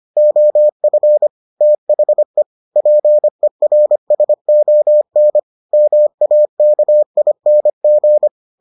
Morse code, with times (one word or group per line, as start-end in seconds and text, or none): 0.2-1.3 OF
1.6-2.4 THE
2.8-5.4 PERSON
5.7-8.3 MAKING